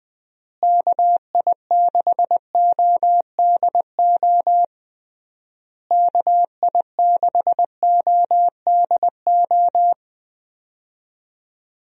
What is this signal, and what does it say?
Morse code, heard as a keyed tone, about 20 wpm, 700 Hz. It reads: KI6ODO KI6ODO
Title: USA Call Signs - 6 Characters - Repeated 3x - 20wpm